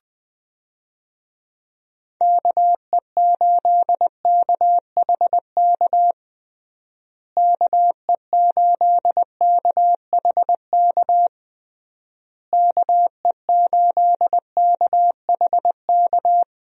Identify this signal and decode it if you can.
Morse code, heard as a keyed tone, about 20 wpm, 700 Hz. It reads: KE8KHK KE8KHK KE8KHK
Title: USA Call Signs - 6 Characters - Repeated 3x - 20wpm